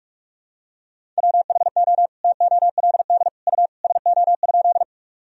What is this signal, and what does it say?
Morse code, heard as a keyed tone, about 41 wpm, 700 Hz. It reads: WHO TOLD U SO?
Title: Sentences from Top 400 Words and Abbreviations 45wpm